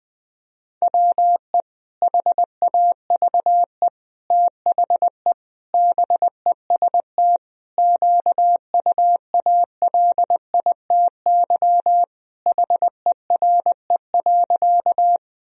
Morse code, heard as a keyed tone, about 19 wpm, 700 Hz. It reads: WE HAVE THE BEST QUALITY HERE.
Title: Sentences from Top 1000 Words - Review Part 4 - 20wpm